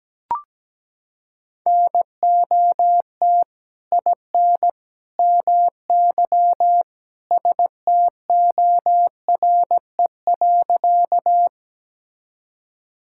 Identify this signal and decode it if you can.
Morse code, heard as a keyed tone, about 17 wpm, 700 Hz. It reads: NOT IN MY STORE.
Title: Sentences from Top 1000 Words - Review Part 5 - 17wpm